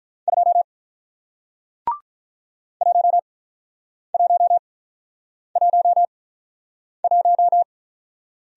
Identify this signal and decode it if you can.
Morse code, heard as a keyed tone, about 48 wpm, spaced as for 9 wpm, 700 Hz. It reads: 2 1111